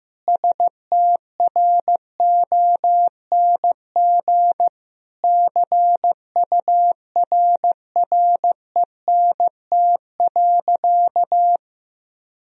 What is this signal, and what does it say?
Morse code, heard as a keyed tone, about 15 wpm, 700 Hz. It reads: STRONG CURRENT.
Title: Sentences from Top 1000 Words - Review Part 4 - 15wpm